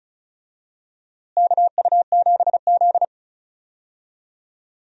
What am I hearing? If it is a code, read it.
Morse code, heard as a keyed tone, about 36 wpm, 700 Hz. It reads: KU7Z
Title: USA Call Signs - 4 Characters - Repeated 3x - 35wpm